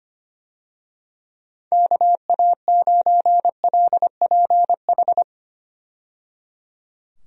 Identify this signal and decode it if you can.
Morse code, heard as a keyed tone, about 25 wpm, 700 Hz. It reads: KA9LPH
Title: USA Call Signs - 6 Characters - 25wpm